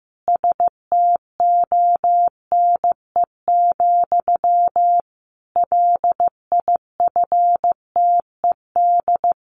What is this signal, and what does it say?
Morse code, heard as a keyed tone, about 15 wpm, 700 Hz. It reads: STONE, LIFTED
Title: A Sound Of Thunder 15wpm